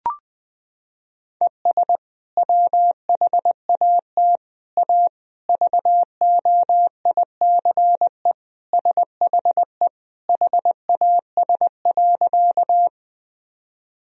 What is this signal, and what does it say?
Morse code, heard as a keyed tone, about 21 wpm, 700 Hz. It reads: ES WHAT A VOICE SHE HAS.
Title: Sentences from Top 500 Words and Abbreviations 20wpm